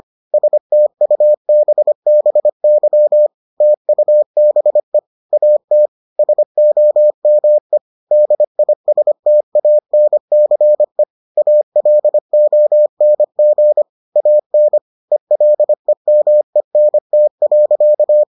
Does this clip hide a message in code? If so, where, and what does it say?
Yes: Morse code from 0.3 to 18.2 s, heard as a keyed tone, about 25 wpm, 600 Hz. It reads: STUBBY TUBE AT SOME DISTANCE ALONG AN ELEMENT.